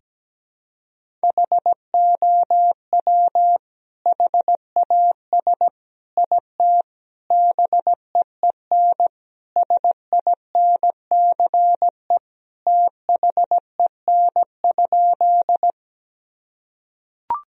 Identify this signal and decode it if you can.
Morse code, heard as a keyed tone, about 17 wpm, 700 Hz. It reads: HOW HAS IT BEEN SINCE THEN?
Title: Sentences from Top 1000 Words - Review Part 1 - 17wpm